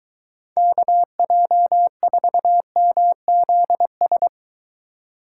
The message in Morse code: KJ4MZS

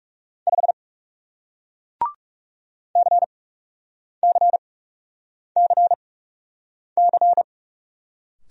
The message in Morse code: 5 CCCC

